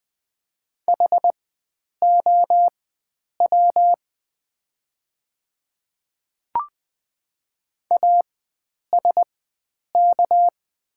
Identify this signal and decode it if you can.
Morse code, heard as a keyed tone, about 20 wpm, spaced as for 9 wpm, 690 Hz. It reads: HOW ASK
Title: 4x Character Spacing - Top 300 Words - 20wpm